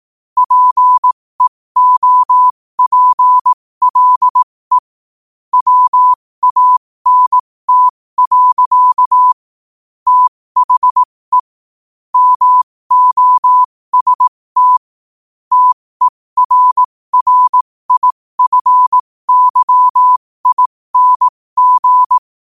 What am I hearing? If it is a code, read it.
Morse code, heard as a keyed tone, about 18 wpm, spaced as for 15 wpm, 1000 Hz. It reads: PEOPLE WANT. THE MOST TERRIFYING
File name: qotdfiles / 15wpm